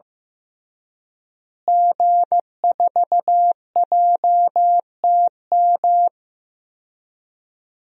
Morse code, heard as a keyed tone, about 15 wpm, 700 Hz. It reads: G4JTM
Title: DXCC-100 Call Signs - 5 Characters - Repeated 3x - 15wpm